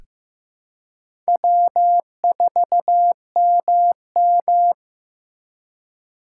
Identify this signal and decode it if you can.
Morse code, heard as a keyed tone, about 15 wpm, 700 Hz. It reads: W4MM
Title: USA Call Signs - 4 Characters - 2x Character Spacing - 15wpm